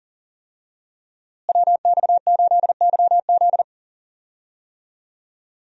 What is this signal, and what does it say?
Morse code, heard as a keyed tone, about 40 wpm, 700 Hz. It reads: WX8YZ
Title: USA Call Signs - 5 Characters - Repeated 2x - 40wpm